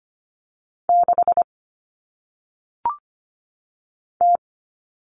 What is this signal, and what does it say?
Morse code, heard as a keyed tone, about 25 wpm, 700 Hz. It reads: 6 T
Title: Common Cut Numbers - 1-Digit 25wpm